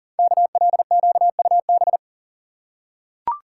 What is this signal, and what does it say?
Morse code, heard as a keyed tone, about 39 wpm, 690 Hz. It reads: KLQUB
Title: Code Groups - 5 Letter:Numbers - 40wpm